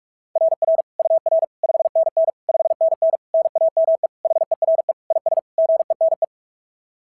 Morse code, heard as a keyed tone, about 41 wpm, 690 Hz. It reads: RR UR 5NN 5NN NAME HERE IS GENE
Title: SOTA QSO Practice 45wpm